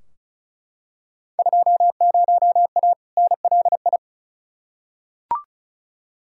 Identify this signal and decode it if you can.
Morse code, heard as a keyed tone, about 36 wpm, 700 Hz. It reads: 20A NLI
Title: Field Day Exchanges Encore - 35wpm